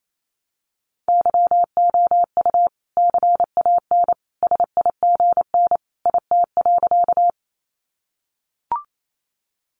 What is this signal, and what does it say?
Morse code, heard as a keyed tone, about 28 wpm, 700 Hz. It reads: YOU CAN SIGN IT.
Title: Sentences from Top 1000 Words 28wpm